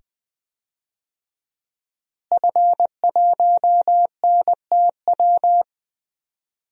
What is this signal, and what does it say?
Morse code, heard as a keyed tone, about 20 wpm, 700 Hz. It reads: F1NTW
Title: DXCC-100 Call Signs - 5 Characters - Repeated 2x - 20wpm